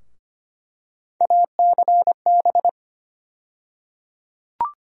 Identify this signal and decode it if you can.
Morse code, heard as a keyed tone, about 25 wpm, 700 Hz. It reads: ACB
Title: USA Call Sign Postfixes - 3 Letters 25wpm